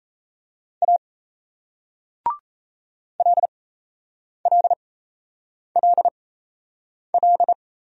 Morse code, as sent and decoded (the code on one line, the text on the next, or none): .- / .-.. .-.. .-.. .-..
A LLLL